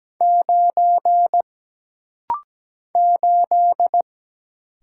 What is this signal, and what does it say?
Morse code, heard as a keyed tone, about 17 wpm, 700 Hz. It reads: 9 8
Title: Single Letter-Number - Rapid-Fire 17wpm